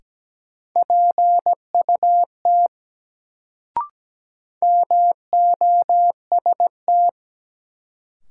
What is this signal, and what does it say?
Morse code, heard as a keyed tone, about 17 wpm, 700 Hz. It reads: PUT MOST
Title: Sentences from Top 200 Words - New Word Drill - 17wpm